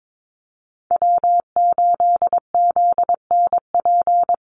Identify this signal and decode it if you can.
Morse code, heard as a keyed tone, about 22 wpm, 700 Hz. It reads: W8ZNP
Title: USA Call Signs Repeated 3x 22wpm